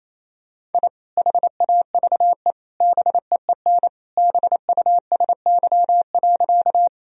Morse code, heard as a keyed tone, about 28 wpm, 700 Hz. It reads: I HAVE BEEN BUSY.
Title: Sentences from Top 1000 Words - Review Part 3 - 28wpm